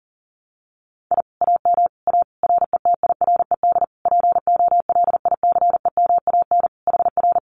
Morse code, heard as a keyed tone, about 40 wpm, 700 Hz. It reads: I AM A RETIRED POLICEMAN HR
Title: Common QSO Phrases Encore 40wpm